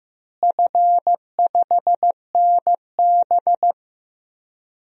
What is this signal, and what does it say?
Morse code, heard as a keyed tone, about 15 wpm, 700 Hz. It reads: F5NB
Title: DXCC-100 Call Signs - 4 Characters - Repeated 2x - 15wpm